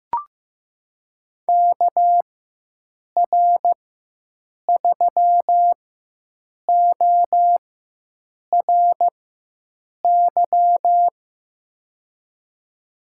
Morse code, heard as a keyed tone, about 15 wpm, spaced as for 7 wpm, 700 Hz. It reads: KR3ORY
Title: USA Call Signs - 4x Character Spacing - 15wpm